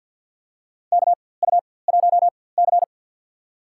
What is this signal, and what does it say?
Morse code, heard as a keyed tone, about 49 wpm, spaced as for 23 wpm, 700 Hz. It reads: KU1C
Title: USA Call Signs - 4x Character Spacing - 50wpm